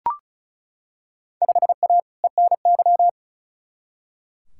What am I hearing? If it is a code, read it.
Morse code, heard as a keyed tone, about 36 wpm, 690 Hz. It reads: 5A ENY
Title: Field Day Exchanges Encore - 35wpm